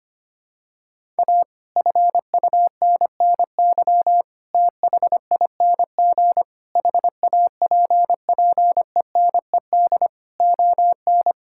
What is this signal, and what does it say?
Morse code, heard as a keyed tone, about 25 wpm, 700 Hz. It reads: A FUNNY THING HAPPENED ON
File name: practice / 25wpm / Sentences with 9-Word Phrases - 25wpm